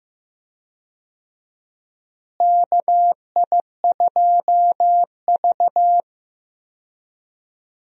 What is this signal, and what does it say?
Morse code, heard as a keyed tone, about 15 wpm, 700 Hz. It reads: KI2V
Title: USA Call Signs - 4 Characters - Repeated 3x - 15wpm